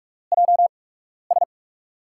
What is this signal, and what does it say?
Morse code, heard as a keyed tone, about 44 wpm, spaced as for 12 wpm, 700 Hz. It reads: JS
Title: USA Call Signs - 5 Characters - 8x Character Spacing - 45wpm